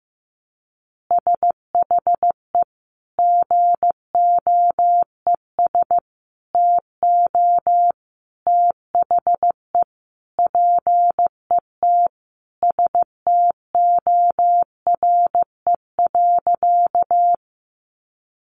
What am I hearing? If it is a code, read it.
Morse code, heard as a keyed tone, about 15 wpm, 700 Hz. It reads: SHE GOES TO THE PET STORE.
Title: Easy Phrases 15wpm